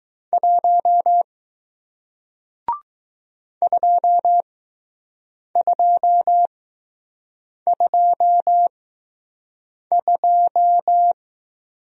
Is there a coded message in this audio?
Yes: Morse code from 0.3 to 11.1 s, heard as a keyed tone, about 20 wpm, spaced as for 6 wpm, 700 Hz. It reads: 1 2222